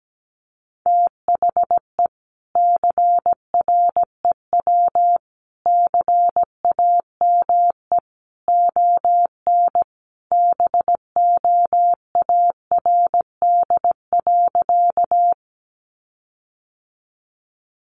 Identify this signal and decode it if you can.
Morse code, heard as a keyed tone, about 17 wpm, 700 Hz. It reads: THE CREW CAME ON BOARD.
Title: Sentences from Top 1500 Words 17wpm